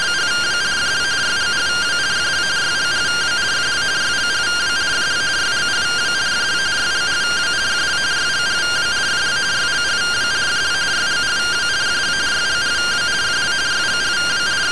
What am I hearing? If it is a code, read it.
Two-tone teleprinter signal, about 170 Hz shift.